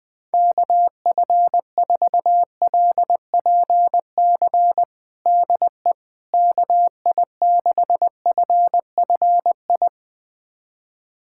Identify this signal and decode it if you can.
Morse code, heard as a keyed tone, about 20 wpm, 700 Hz. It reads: KF4LPC DE KI6FFI
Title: USA Callsign DE Callsign - 6 Characters - 20wpm